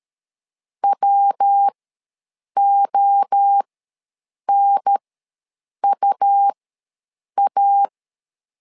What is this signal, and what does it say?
Morse code, heard as a keyed tone, about 13 wpm, spaced as for 7 wpm, 800 Hz. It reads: WONUA
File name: LEZIONI / 3 MWRO / 3 MWRO ws 12wpm 1914